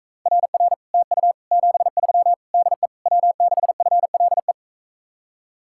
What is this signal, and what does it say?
Morse code, heard as a keyed tone, about 45 wpm, 700 Hz. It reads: RR TU 73 DE W6FLE